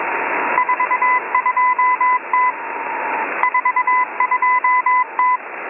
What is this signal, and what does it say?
Morse code, heard as a keyed tone, about 21 wpm, 1000 Hz, with radio noise. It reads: 42T 42T
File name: M01- -1810utc-4589usb-msg-byDanix